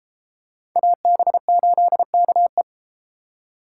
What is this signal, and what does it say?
Morse code, heard as a keyed tone, about 31 wpm, 700 Hz. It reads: AB8KE